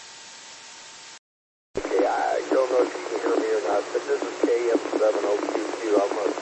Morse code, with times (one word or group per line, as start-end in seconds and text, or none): none